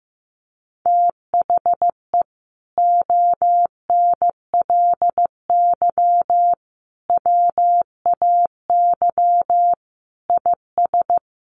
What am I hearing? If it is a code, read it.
Morse code, heard as a keyed tone, about 15 wpm, 700 Hz. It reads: THE ONLY WAY IS
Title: Sentences from Top 1500 Words 15wpm